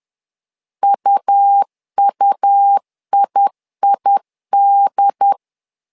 Morse code, heard as a keyed tone, about 10 wpm, 800 Hz. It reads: UUIID
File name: LEZIONI / 2 UHDN / 2 UHD 10wpm 1826